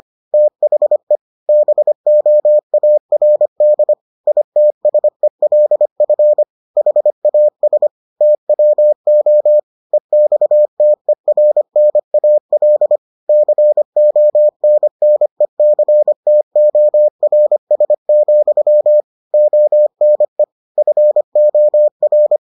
Morse code, heard as a keyed tone, about 25 wpm, 600 Hz. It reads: THE BOARD ITSELF HAS TWO EXTERNAL CONNECTORS, ONE FOR